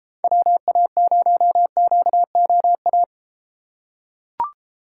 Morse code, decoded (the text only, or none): WA0QOA